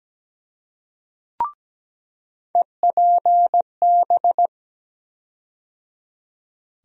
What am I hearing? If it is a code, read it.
Morse code, heard as a keyed tone, about 17 wpm, 700 Hz. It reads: EPB
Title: USA Call Sign Postfixes - 3 Letters 17wpm